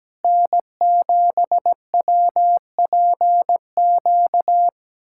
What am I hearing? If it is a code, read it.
Morse code, heard as a keyed tone, about 17 wpm, 700 Hz. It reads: N7WPQ
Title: USA Callsign DE Callsign 17wpm